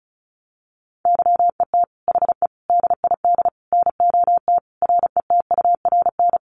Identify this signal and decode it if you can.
Morse code, heard as a keyed tone, about 34 wpm, 700 Hz. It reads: YET HE DID NOT RETURN
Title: Sentences from Top 800 Words 35wpm